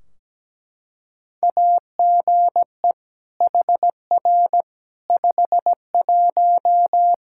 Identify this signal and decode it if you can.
Morse code, heard as a keyed tone, about 17 wpm, 700 Hz. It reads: AGE HR 51
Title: Practice QSO - Exchange 3 - More Info - 17wpm